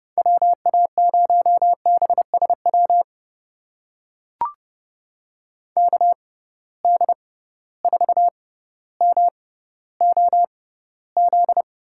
Morse code, heard as a keyed tone, about 30 wpm, spaced as for 15 wpm, 700 Hz. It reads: WA0BSW KD4MOZ
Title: USA Call Signs - 6 Characters - 6x Character Spacing - 30wpm